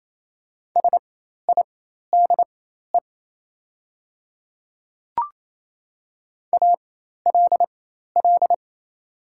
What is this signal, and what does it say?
Morse code, heard as a keyed tone, about 28 wpm, spaced as for 13 wpm, 700 Hz. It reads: SIDE ALL
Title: 4x Character Spacing - Top 100 Words - 28wpm